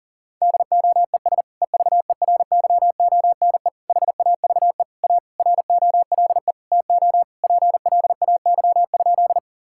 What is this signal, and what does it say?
Morse code, heard as a keyed tone, about 40 wpm, 700 Hz. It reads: DOES EVERYONE HAVE A ROLE TO PLAY?
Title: Sentences from Top 1000 Words - Review Part 6 - 40wpm